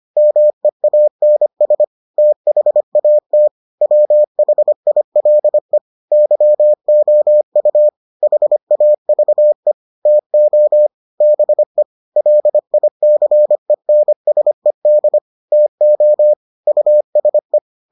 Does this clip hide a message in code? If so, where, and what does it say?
Yes: Morse code from 0.2 to 17.6 s, heard as a keyed tone, about 25 wpm, 600 Hz. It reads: MEANS THAT WHILE YOU HAVE TO BE LICENSED TO USE